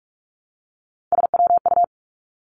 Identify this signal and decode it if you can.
Morse code, heard as a keyed tone, about 46 wpm, 700 Hz. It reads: SWU
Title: Three Challenging Letter-Number Combinations 45wpm